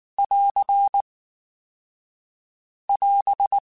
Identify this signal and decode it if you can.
Morse code, heard as a keyed tone, about 19 wpm, 800 Hz. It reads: <AR> <AS>
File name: Spaced / Chuckles Spaced-mp3s / BrainMouth sp @18